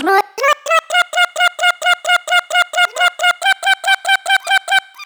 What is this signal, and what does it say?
There is no code.